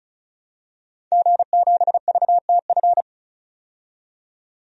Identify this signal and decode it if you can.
Morse code, heard as a keyed tone, about 36 wpm, 700 Hz. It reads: G7VTF
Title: DXCC-100 Call Signs - 5 Characters - 35wpm